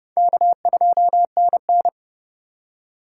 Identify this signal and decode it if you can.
Morse code, heard as a keyed tone, about 30 wpm, 700 Hz. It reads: K2NN
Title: USA Call Signs - 4 Characters - 3x Character Spacing - 30wpm